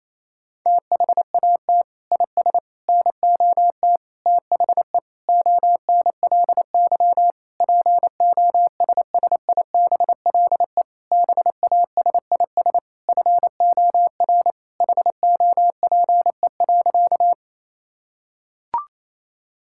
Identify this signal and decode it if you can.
Morse code, heard as a keyed tone, about 28 wpm, 700 Hz. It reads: THAT IS NOT THE ONLY POSSIBLE BASIS FOR HOPE.
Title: Sentences from Top 1000 Words - Review Part 5 - 28wpm